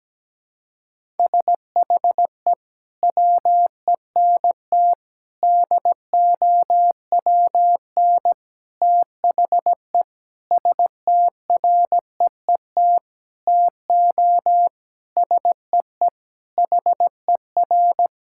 Morse code, heard as a keyed tone, about 17 wpm, 700 Hz. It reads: SHE WENT DOWN THE STREET TO SEE HER
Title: Sentences from Top 1000 Words - Review Part 5 - 17wpm